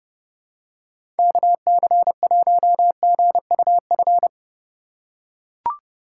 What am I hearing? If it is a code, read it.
Morse code, heard as a keyed tone, about 30 wpm, 700 Hz. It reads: KC1GUF
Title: USA Call Signs - 2x Character Spacing - 30wpm